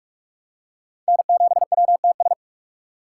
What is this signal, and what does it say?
Morse code, heard as a keyed tone, about 47 wpm, 700 Hz. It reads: N7WTS
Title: USA Call Signs - 4x Character Spacing - 45wpm